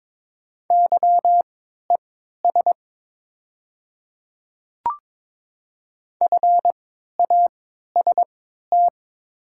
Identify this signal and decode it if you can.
Morse code, heard as a keyed tone, about 22 wpm, spaced as for 12 wpm, 700 Hz. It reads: YES FAST
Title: 3x Character Spacing - Top 500 Words - 22wpm